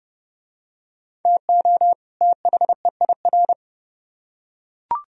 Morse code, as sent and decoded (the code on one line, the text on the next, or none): - --- / - .... .. .-.
TO THEIR